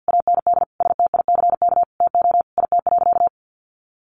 Morse code, heard as a keyed tone, about 50 wpm, 700 Hz. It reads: AND STICK TO IT.
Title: Sentences from Top 1200 Words 50wpm